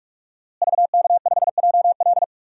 Morse code, heard as a keyed tone, about 47 wpm, 700 Hz. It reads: VK5JL
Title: DXCC-100 Call Signs - 5 Characters - Repeated 3x - 45wpm